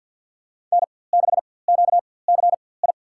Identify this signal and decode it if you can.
Morse code, heard as a keyed tone, about 51 wpm, spaced as for 23 wpm, 700 Hz. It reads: N6QCI
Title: USA Call Signs - 4x Character Spacing - 50wpm